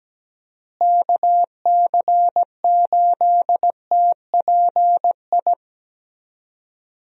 Morse code, heard as a keyed tone, about 17 wpm, 700 Hz. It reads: KC8TPI
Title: USA Call Signs - 6 Characters - Repeated 2x - 17wpm